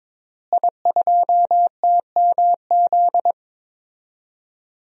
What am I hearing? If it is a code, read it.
Morse code, heard as a keyed tone, about 22 wpm, 700 Hz. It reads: I2TMZ
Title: DXCC-100 Call Signs - 5 Characters - Repeated 3x - 22wpm